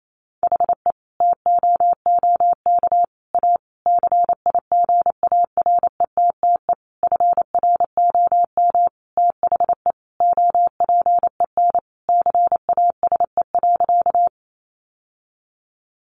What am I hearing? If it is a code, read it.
Morse code, heard as a keyed tone, about 28 wpm, 700 Hz. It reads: HE TOOK A CIGARETTE FROM THE OPEN CASE.
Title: Sentences from Top 1000 Words 28wpm